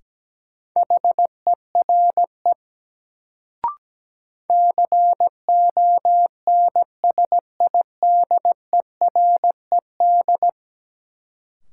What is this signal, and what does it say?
Morse code, heard as a keyed tone, about 17 wpm, 700 Hz. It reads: HERE CONSIDERED
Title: Sets of 5 Words - New Word Drill - 17wpm